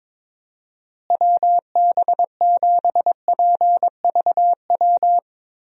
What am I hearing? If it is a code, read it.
Morse code, heard as a keyed tone, about 22 wpm, 700 Hz. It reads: WB7PVW
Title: USA Call Signs - 6 Characters - Repeated 2x - 22wpm